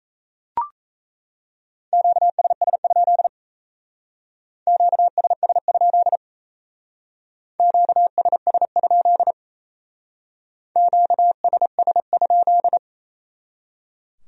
Morse code, heard as a keyed tone, about 36 wpm, 700 Hz. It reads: QSS? QSS? QSS? QSS?